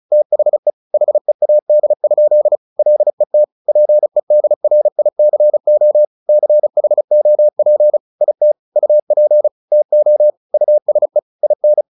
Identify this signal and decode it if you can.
Morse code, heard as a keyed tone, about 34 wpm, 600 Hz. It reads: THE HEAD? LET PEDRICO CHOP IT UP TO USE IN
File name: the old man and the sea 2 35wpm